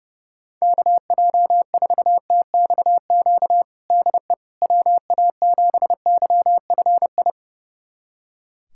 Morse code, heard as a keyed tone, about 31 wpm, 700 Hz. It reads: KJ4TXQ DE WA7YFI
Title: USA Callsign DE Callsign - 6 Characters - 30wpm